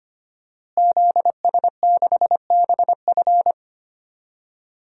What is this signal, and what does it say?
Morse code, heard as a keyed tone, about 25 wpm, 700 Hz. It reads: ZS6BF
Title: DXCC-100 Call Signs - 5 Characters - Repeated 2x - 25wpm